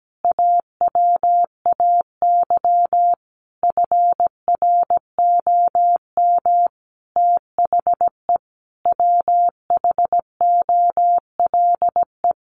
Morse code, heard as a keyed tone, about 17 wpm, 700 Hz. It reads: AWAY FROM THE WHOLE
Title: Sentences from Top 1500 Words 17wpm